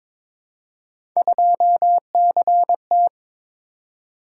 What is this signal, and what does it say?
Morse code, heard as a keyed tone, about 22 wpm, 700 Hz. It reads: 2CT
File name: practice / 22wpm / Three Letter-Number Combinations - 22wpm